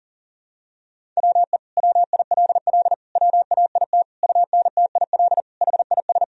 Morse code, heard as a keyed tone, about 37 wpm, 700 Hz. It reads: WE WILL WAIT UNTIL HIS
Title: Sentences from Top 1000 Words - Review Part 1 - 40wpm